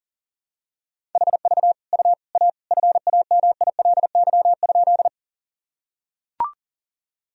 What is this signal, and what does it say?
Morse code, heard as a keyed tone, about 37 wpm, 700 Hz. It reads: HV U A FAMILY?
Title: Sentences from Top 400 Words and Abbreviations 40wpm